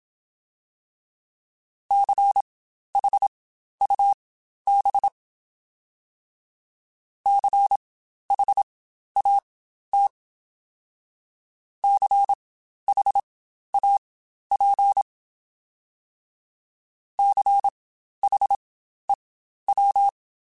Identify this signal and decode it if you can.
Morse code, heard as a keyed tone, about 27 wpm, spaced as for 12 wpm, 800 Hz. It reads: CHUB CHAT CHAP CHEW